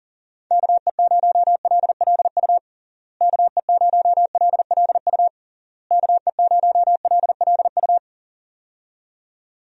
Morse code, heard as a keyed tone, about 40 wpm, 700 Hz. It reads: KE0LLU KE0LLU KE0LLU
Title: USA Call Signs - 6 Characters - Repeated 3x - 40wpm